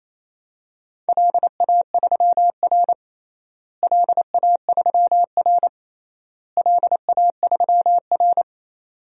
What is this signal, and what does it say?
Morse code, heard as a keyed tone, about 28 wpm, 700 Hz. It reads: LA3R LA3R LA3R
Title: DXCC-100 Call Signs - 4 Characters - Repeated 3x - 28wpm